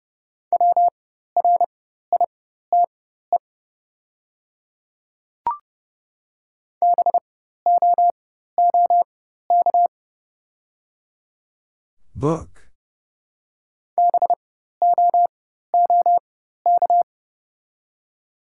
Morse code, heard as a keyed tone, about 29 wpm, spaced as for 14 wpm, 700 Hz. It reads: WRITE BOOK BOOK
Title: 4x Character Spacing - Top 300 Words - 30wpm